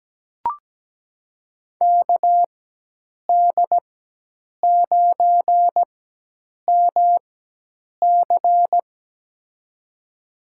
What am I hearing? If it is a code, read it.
Morse code, heard as a keyed tone, about 17 wpm, spaced as for 8 wpm, 700 Hz. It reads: KD9MC
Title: USA Call Signs - 5 Characters - 4x Character Spacing - 17wpm